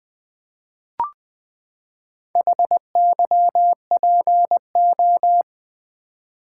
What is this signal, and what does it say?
Morse code, heard as a keyed tone, about 20 wpm, 700 Hz. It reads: HYPO
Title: Build A Word - 4 Letters At A Time - 10-Letter Words - 20wpm